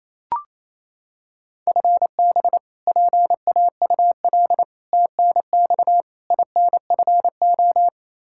Morse code, heard as a keyed tone, about 28 wpm, 700 Hz. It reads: FB PAUL TNX INFO
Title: Practice QSO - Exchange 3 - More Info - 28wpm